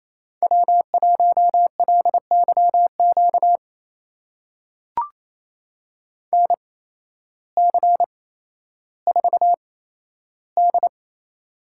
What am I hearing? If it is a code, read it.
Morse code, heard as a keyed tone, about 28 wpm, spaced as for 13 wpm, 700 Hz. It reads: W1LYQ NC4D
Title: USA Call Signs - 8x Character Spacing - 28wpm